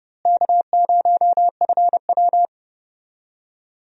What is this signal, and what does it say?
Morse code, heard as a keyed tone, about 30 wpm, 700 Hz. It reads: K0FW
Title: USA Call Signs - 4 Characters - Repeated 3x - 30wpm